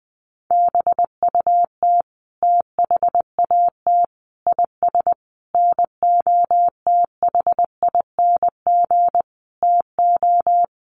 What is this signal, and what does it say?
Morse code, heard as a keyed tone, about 20 wpm, 700 Hz. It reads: BUT THAT IS NOTHING TO
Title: Sentences from Top 500 Words 20wpm